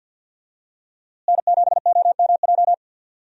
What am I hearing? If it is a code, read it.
Morse code, heard as a keyed tone, about 52 wpm, 700 Hz. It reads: N7OMJ
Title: USA Call Signs - 5 Characters - Repeated 2x - 50wpm